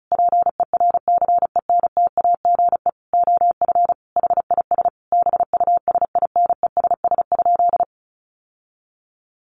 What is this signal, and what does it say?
Morse code, heard as a keyed tone, about 34 wpm, 700 Hz. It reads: PERCENTAGE OF HIS BUSINESS?
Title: Sentences from Top 1300 Words 35wpm